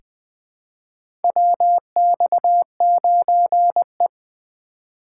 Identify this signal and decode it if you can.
Morse code, heard as a keyed tone, about 21 wpm, 700 Hz. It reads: WX9E
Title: USA Call Signs - 4 Characters - 3x Character Spacing - 20wpm